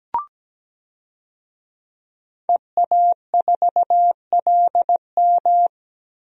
Morse code, heard as a keyed tone, about 17 wpm, 700 Hz. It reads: EA4LM
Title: DXCC-100 Call Signs - 5 Characters - Repeated 3x - 17wpm